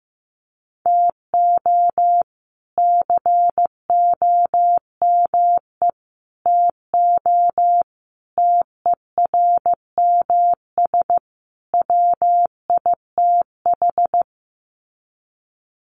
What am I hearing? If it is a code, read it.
Morse code, heard as a keyed tone, about 15 wpm, 700 Hz. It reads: TO COME TO TERMS WITH
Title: Sets of 5 Words Encore 15wpm